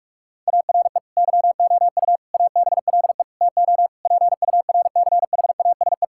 Morse code, heard as a keyed tone, about 41 wpm, 700 Hz. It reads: ARE YOU ABLE TO PURCHASE
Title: Sentences from Top 1000 Words - Review Part 5 - 45wpm